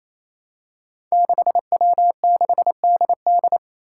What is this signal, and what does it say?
Morse code, heard as a keyed tone, about 28 wpm, 700 Hz. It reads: 6W6DD
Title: Five Challenging Letter-Number Combinations 28wpm